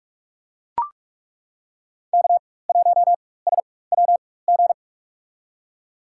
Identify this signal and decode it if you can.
Morse code, heard as a keyed tone, about 42 wpm, spaced as for 20 wpm, 700 Hz. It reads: K1SWG
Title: USA Call Signs - 4x Character Spacing - 45wpm